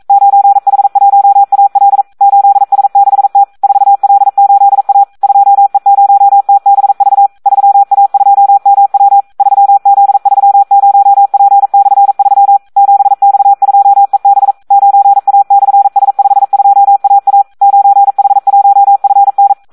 Morse code, heard as a keyed tone, about 42 wpm, 800 Hz. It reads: 9H1AL 8S6T 4L8A 2E0TBV 3A2MW 3Z30P/3 7X2EB 9A/S52AA 9H1FN
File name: Audio practica CW / SD 1 42